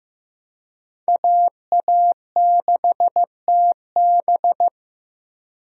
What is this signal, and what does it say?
Morse code, heard as a keyed tone, about 15 wpm, 700 Hz. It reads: AA6TB
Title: USA Call Signs - 5 Characters - Repeated 2x - 15wpm